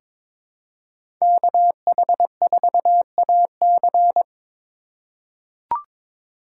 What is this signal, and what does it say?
Morse code, heard as a keyed tone, about 22 wpm, 700 Hz. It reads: KH4AC
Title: USA Call Signs - 5 Characters - 3x Character Spacing - 22wpm